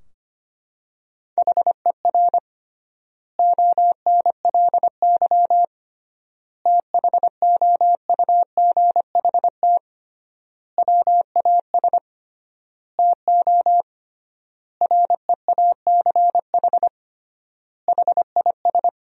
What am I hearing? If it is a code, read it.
Morse code, heard as a keyed tone, about 25 wpm, 700 Hz. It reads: HER ONLY THOUGHT WAS TO REACH HIS
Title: 3x Word-Spacing - Sentences from Top 400 Words - 25wpm